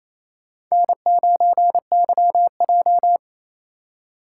Morse code, heard as a keyed tone, about 28 wpm, 700 Hz. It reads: N9YJ
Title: USA Call Signs - 4 Characters - 3x Character Spacing - 28wpm